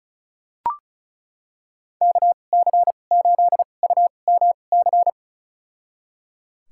Morse code, heard as a keyed tone, about 36 wpm, spaced as for 25 wpm, 700 Hz. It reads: KC8UMC
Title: USA Call Signs - 6 Characters - 2x Character Spacing - 35wpm